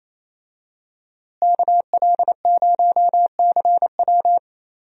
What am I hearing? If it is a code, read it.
Morse code, heard as a keyed tone, about 28 wpm, 700 Hz. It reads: KL0CW
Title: USA Call Signs - 5 Characters - Repeated 3x - 28wpm